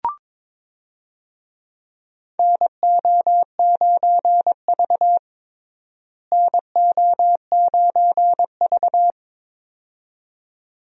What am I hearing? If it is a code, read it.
Morse code, heard as a keyed tone, about 21 wpm, 700 Hz. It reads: NO9V NO9V
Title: USA Call Signs - 4 Characters - Repeated 2x - 22wpm